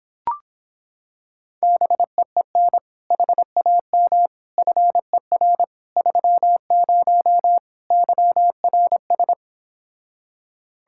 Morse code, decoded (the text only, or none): BEEN HAM FER 30 YRS